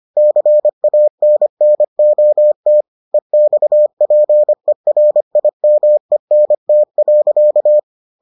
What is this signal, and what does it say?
Morse code, heard as a keyed tone, about 25 wpm, 600 Hz. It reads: CANNOT EXPERIMENT.